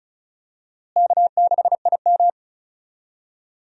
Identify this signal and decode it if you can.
Morse code, heard as a keyed tone, about 36 wpm, 700 Hz. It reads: K6IM
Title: USA Call Signs - 4 Characters - Repeated 3x - 35wpm